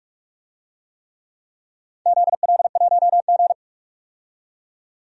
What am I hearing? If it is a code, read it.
Morse code, heard as a keyed tone, about 44 wpm, 700 Hz. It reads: ZL1G
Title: DXCC-100 Call Signs - 4 Characters - 45wpm